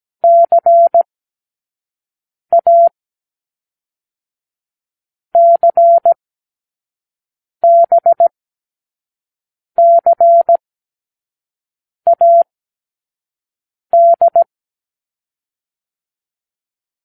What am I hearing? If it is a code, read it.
Morse code, heard as a keyed tone, about 17 wpm, 680 Hz. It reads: C A C B C A D
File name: k7qo / Lesson6(A-D)